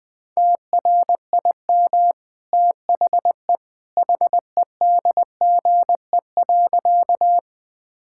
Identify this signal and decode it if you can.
Morse code, heard as a keyed tone, about 20 wpm, 700 Hz. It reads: TRIM THE HEDGE.
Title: Easy Sentences with Hints - 20wpm